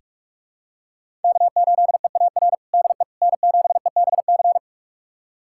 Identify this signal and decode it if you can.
Morse code, heard as a keyed tone, about 45 wpm, 700 Hz. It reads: K8EAR DE N7EBC
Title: USA Callsign DE Callsign - 5 Characters - 45wpm